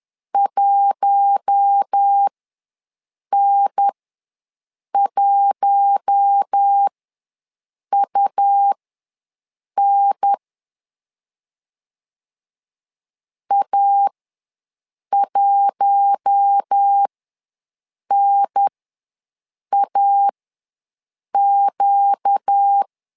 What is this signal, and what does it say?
Morse code, heard as a keyed tone, about 10 wpm, spaced as for 6 wpm, 800 Hz. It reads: 1N1UN A1NAQ
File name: LEZIONI / 4 Q1 ws 10wpm 2538